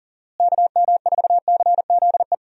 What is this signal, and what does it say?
Morse code, heard as a keyed tone, about 40 wpm, 700 Hz. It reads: KM4CZE
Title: USA Call Signs - 6 Characters - Repeated 3x - 40wpm